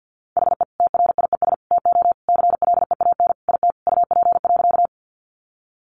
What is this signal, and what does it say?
Morse code, heard as a keyed tone, about 50 wpm, 700 Hz. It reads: HE TRIES TO CLEAN IT UP.
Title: Additional Easy Phrases 50wpm